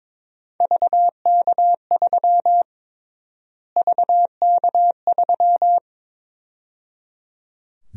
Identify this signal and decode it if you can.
Morse code, heard as a keyed tone, about 22 wpm, 700 Hz. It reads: VK3 VK3
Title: USA Call Sign Prefixes - Repeated 2x - 22wpm